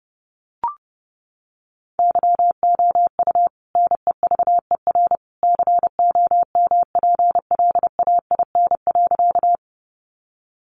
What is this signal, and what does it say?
Morse code, heard as a keyed tone, about 28 wpm, 700 Hz. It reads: YOU NEVER COMPLAIN.
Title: Sentences from Top 1300 Words 30wpm